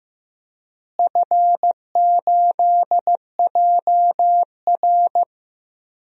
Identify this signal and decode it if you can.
Morse code, heard as a keyed tone, about 16 wpm, 700 Hz. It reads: F8JR
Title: DXCC-100 Call Signs - 4 Characters - Repeated 3x - 15wpm